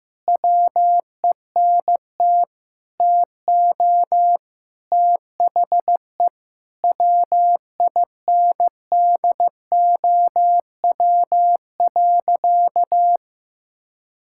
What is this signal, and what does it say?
Morse code, heard as a keyed tone, about 15 wpm, 700 Hz. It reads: WENT TO THE WINDOW.
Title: Sentences from Top 1600 Words 15wpm